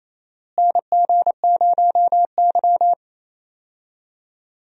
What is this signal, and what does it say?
Morse code, heard as a keyed tone, about 27 wpm, 700 Hz. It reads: NG0Y
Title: USA Call Signs - 4 Characters - Repeated 2x - 28wpm